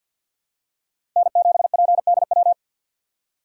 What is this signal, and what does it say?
Morse code, heard as a keyed tone, about 49 wpm, 700 Hz. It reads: N7PDW
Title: USA Call Signs - 5 Characters - 3x Character Spacing - 50wpm